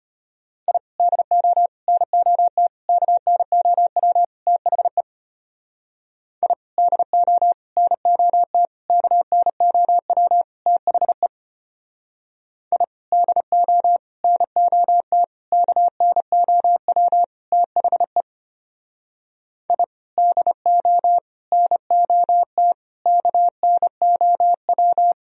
Morse code, heard as a keyed tone, about 31 wpm, 700 Hz. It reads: I DO NOT KNOW THE I DO NOT KNOW THE I DO NOT KNOW THE I DO NOT KNOW
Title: Speed-Racing - Sets of 5 Words 25wpm